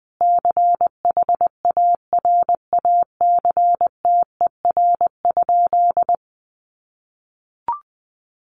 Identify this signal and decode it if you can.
Morse code, heard as a keyed tone, about 20 wpm, 700 Hz. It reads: CHARACTER?
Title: Sentences from Top 1200 Words 20wpm